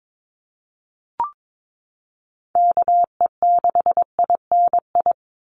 Morse code, heard as a keyed tone, about 22 wpm, 700 Hz. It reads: KE6INI
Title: USA Call Signs 22wpm